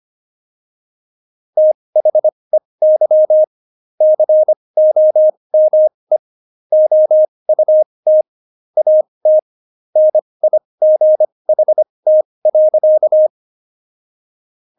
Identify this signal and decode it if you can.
Morse code, heard as a keyed tone, about 25 wpm, spaced as for 20 wpm, 620 Hz. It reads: THEY COME OUT AT NIGHT.
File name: Jokes-25-char-20-eff